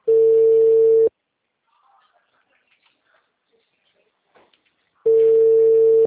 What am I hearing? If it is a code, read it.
Ringback tone (China, 450 Hz).